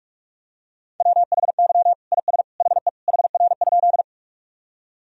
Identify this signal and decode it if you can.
Morse code, heard as a keyed tone, about 47 wpm, 700 Hz. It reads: WHY IS HE HR?
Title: Sentences from Top 200 Words and Abbreviations 45wpm